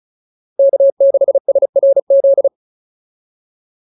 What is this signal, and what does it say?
Morse code, heard as a keyed tone, about 34 wpm, 590 Hz. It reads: K6SRZ
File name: POTA109 35